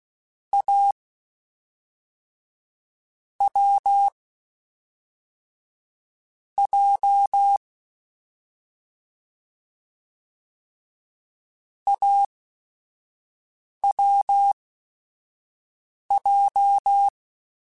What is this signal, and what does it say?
Morse code, heard as a keyed tone, about 16 wpm, spaced as for 4 wpm, 800 Hz. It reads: AWJ AWJ
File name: Short Runs / Short Runs-mp3s / AWJ @5